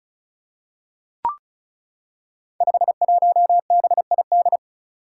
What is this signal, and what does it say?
Morse code, heard as a keyed tone, about 36 wpm, 700 Hz. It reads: 51BID
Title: Five Challenging Letter-Number Combinations 35wpm